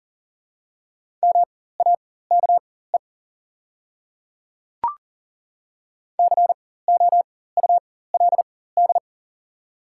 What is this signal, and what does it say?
Morse code, heard as a keyed tone, about 38 wpm, spaced as for 18 wpm, 700 Hz. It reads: MAKE COULD
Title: 4x Character Spacing - Top 100 Words - 40wpm